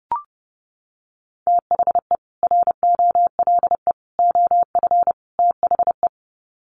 Morse code, heard as a keyed tone, about 30 wpm, 700 Hz. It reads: THE ROLE OF THE
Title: Sets of 4 Words 30wpm